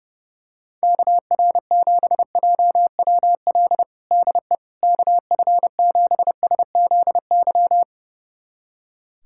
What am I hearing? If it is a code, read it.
Morse code, heard as a keyed tone, about 30 wpm, 700 Hz. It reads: KR7JWL DE KF7SZY
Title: USA Callsign DE Callsign - 6 Characters - 30wpm